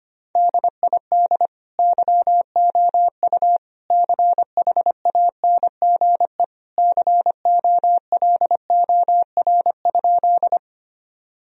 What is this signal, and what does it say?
Morse code, heard as a keyed tone, about 25 wpm, 700 Hz. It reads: DID YOU CHANGE COLOR?
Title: Sentences from Top 1000 Words - Review Part 2 - 25wpm